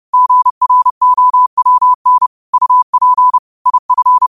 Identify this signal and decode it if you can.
Morse code, heard as a keyed tone, about 30 wpm, 1000 Hz. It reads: GROWN UP IF
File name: qotdfiles / 30wpm